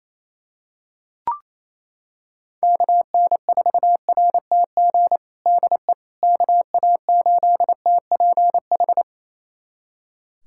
Morse code, heard as a keyed tone, about 28 wpm, 700 Hz. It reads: KN4RTG DE KA8TPH
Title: USA Callsign DE Callsign - 6 Characters - 28wpm